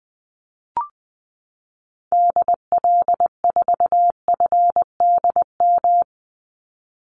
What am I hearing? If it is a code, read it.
Morse code, heard as a keyed tone, about 20 wpm, 700 Hz. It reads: DL4FDM
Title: Call Signs Repeated 2x - 20wpm